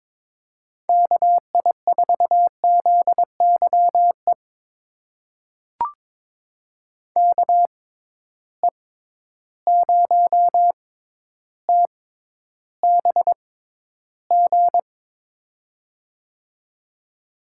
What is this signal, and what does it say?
Morse code, heard as a keyed tone, about 22 wpm, spaced as for 11 wpm, 700 Hz. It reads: KI4ZYE KE0TBG
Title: USA Call Signs - 6 Characters - 6x Character Spacing - 22wpm